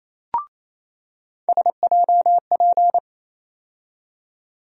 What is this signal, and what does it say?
Morse code, heard as a keyed tone, about 28 wpm, 700 Hz. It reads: SJP
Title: USA Call Sign Postfixes - 3 Letters 28wpm